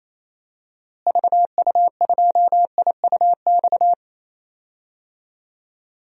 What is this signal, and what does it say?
Morse code, heard as a keyed tone, about 27 wpm, 700 Hz. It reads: VU2IUX
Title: DXCC-100 Call Signs - 6 Characters - Repeated 2x - 28wpm